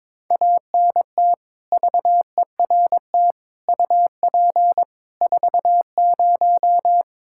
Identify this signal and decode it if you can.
Morse code, heard as a keyed tone, about 22 wpm, 700 Hz. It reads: ANT VERT UP 40
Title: Practice QSO - Exchange 2 - Info - 22wpm